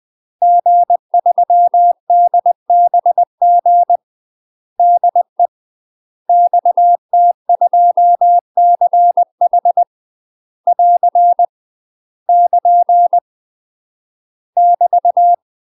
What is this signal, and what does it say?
Morse code, heard as a keyed tone, about 20 wpm, 700 Hz. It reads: G3DBG DE XT2CH <AR> <KN> <BT>
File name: UK MASTER QSOs TRAINING / 20WPM / UK MASTER QSOs 0034